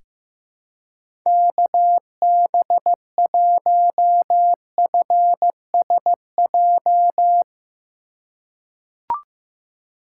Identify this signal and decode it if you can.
Morse code, heard as a keyed tone, about 15 wpm, 700 Hz. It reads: KB1FSJ
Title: USA Call Signs - 6 Characters - 4x Character Spacing - 15wpm